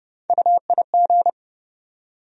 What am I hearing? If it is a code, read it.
Morse code, heard as a keyed tone, about 30 wpm, 700 Hz. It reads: UIG
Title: USA Call Sign Postfixes - 3 Letters 30wpm